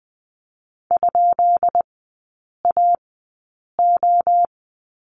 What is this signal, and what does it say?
Morse code, heard as a keyed tone, about 20 wpm, spaced as for 8 wpm, 690 Hz. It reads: ?AO
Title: 145 Lesson 24e - Letter y - Three Characters 20wpm